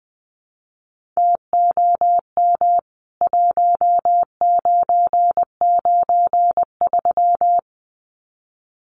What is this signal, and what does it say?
Morse code, heard as a keyed tone, about 20 wpm, 700 Hz. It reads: TOM 1993
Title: CWT Exchange 20wpm